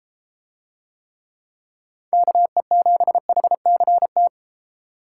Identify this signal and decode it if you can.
Morse code, heard as a keyed tone, about 31 wpm, 700 Hz. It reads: KE7HCT